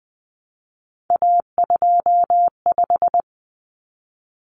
Morse code, heard as a keyed tone, about 20 wpm, 700 Hz. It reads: A25
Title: Common Cut Numbers - 3-Digit 20wpm